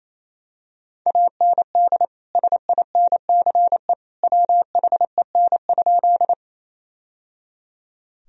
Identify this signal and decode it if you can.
Morse code, heard as a keyed tone, about 28 wpm, 700 Hz. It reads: AND SINCE WHEN?
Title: Sentences from Top 1000 Words - Review Part 1 - 28wpm